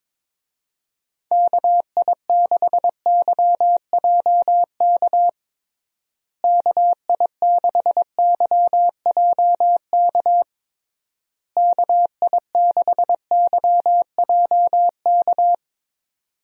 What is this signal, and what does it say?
Morse code, heard as a keyed tone, about 22 wpm, 700 Hz. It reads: KI6YJK KI6YJK KI6YJK
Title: USA Call Signs - 6 Characters - Repeated 3x - 22wpm